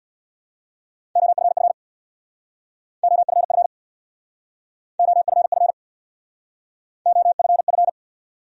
Morse code, heard as a keyed tone, about 65 wpm, 700 Hz. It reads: OFF OFF OFF OFF